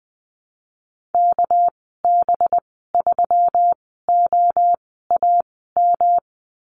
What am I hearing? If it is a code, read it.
Morse code, heard as a keyed tone, about 20 wpm, spaced as for 15 wpm, 700 Hz. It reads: KB3OAM
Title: 200 Lesson 32g - Letter k - Call Signs 20wpm